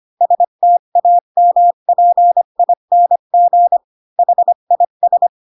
Morse code, heard as a keyed tone, about 25 wpm, 700 Hz. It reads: STAMPING HIS
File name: Pinocchio / Pinocchio ch5 20 25 ik1hge